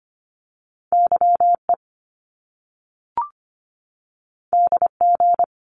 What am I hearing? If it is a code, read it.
Morse code, heard as a keyed tone, about 25 wpm, 700 Hz. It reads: YE DG
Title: Top Two Letter Combinations in Words 25wpm